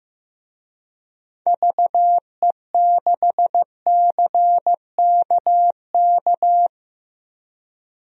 Morse code, heard as a keyed tone, about 15 wpm, 700 Hz. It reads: VE6CKK
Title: DXCC-100 Call Signs - 6 Characters - Repeated 3x - 15wpm